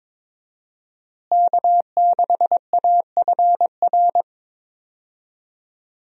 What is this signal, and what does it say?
Morse code, heard as a keyed tone, about 21 wpm, 700 Hz. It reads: K6AFR
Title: USA Call Signs - 5 Characters - Repeated 3x - 22wpm